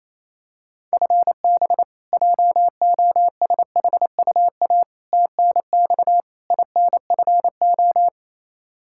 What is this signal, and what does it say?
Morse code, heard as a keyed tone, about 28 wpm, 700 Hz. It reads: FB JOSHUA TNX INFO
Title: K1USN SST QSO Practice 28wpm